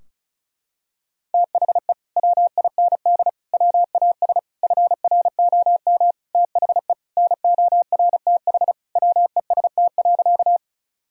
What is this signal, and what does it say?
Morse code, heard as a keyed tone, about 34 wpm, 700 Hz. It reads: THE WIND WAS FROM THE NORTH WEST.
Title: Sentences from Top 1000 Words - Review Part 2 - 35wpm